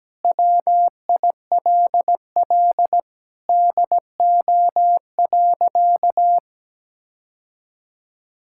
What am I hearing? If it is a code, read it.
Morse code, heard as a keyed tone, about 17 wpm, 700 Hz. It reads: WILL DO.
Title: Sentences from Top 1600 Words 17wpm